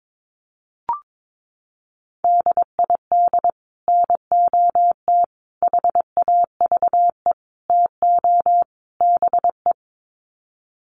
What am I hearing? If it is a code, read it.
Morse code, heard as a keyed tone, about 22 wpm, 700 Hz. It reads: DID NOT HAVE TO BE
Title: Sets of 5 Words Encore2 22wpm